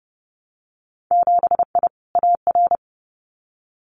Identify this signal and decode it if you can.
Morse code, heard as a keyed tone, about 30 wpm, 700 Hz. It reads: 7I AR
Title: Winter Field Day Exchanges Encore 30wpm